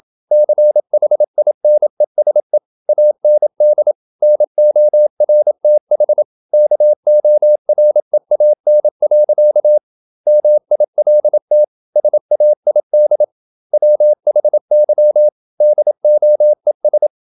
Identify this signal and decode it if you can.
Morse code, heard as a keyed tone, about 27 wpm, 600 Hz. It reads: CHINESE AND NORTH KOREAN. MILT SAID WHY DOES
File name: Never 0027wpm